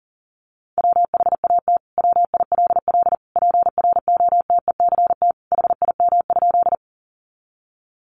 Morse code, heard as a keyed tone, about 40 wpm, 700 Hz. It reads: WHAT WILL PROTECT HIM?
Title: Sentences from Top 900 Words 40wpm